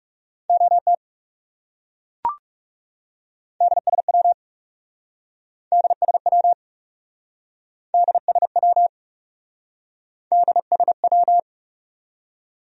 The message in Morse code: OT DSW DSW DSW DSW

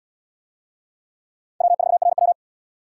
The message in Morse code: W4RW

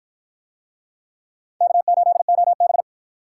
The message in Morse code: K8OB

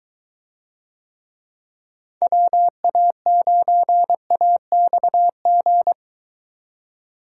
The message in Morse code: WA9AXG

